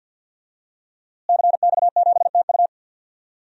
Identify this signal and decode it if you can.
Morse code, heard as a keyed tone, about 48 wpm, 700 Hz. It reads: CX7TU
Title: DXCC-100 Call Signs - 5 Characters - Repeated 3x - 50wpm